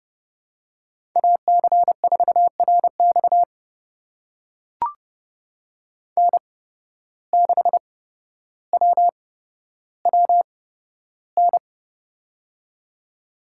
Morse code, heard as a keyed tone, about 31 wpm, spaced as for 13 wpm, 700 Hz. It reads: AC4RX N6WWN
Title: USA Call Signs - 5 Characters - 8x Character Spacing - 30wpm